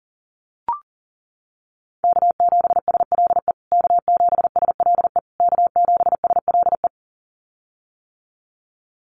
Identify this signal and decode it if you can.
Morse code, heard as a keyed tone, about 40 wpm, 700 Hz. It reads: K7SLE K7SLE K7SLE
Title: USA Call Signs Repeated 3x 40wpm